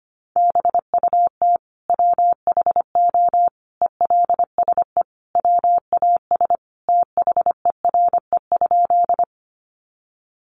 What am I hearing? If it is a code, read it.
Morse code, heard as a keyed tone, about 25 wpm, 700 Hz. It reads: BUT WHO ELSE WAS THERE?
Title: Sentences from Top 800 Words 25wpm